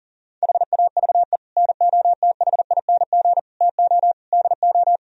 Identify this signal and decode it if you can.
Morse code, heard as a keyed tone, about 51 wpm, 700 Hz. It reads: HAVE NOTHING TO DO